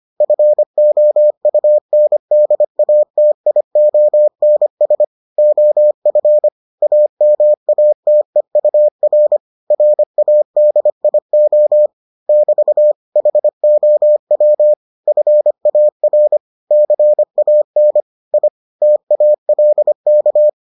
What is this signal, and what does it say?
Morse code, heard as a keyed tone, about 25 wpm, 600 Hz. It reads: FOUNDATIONS OF AMATEUR RADIO <BT> HOW FAR CAN I TALK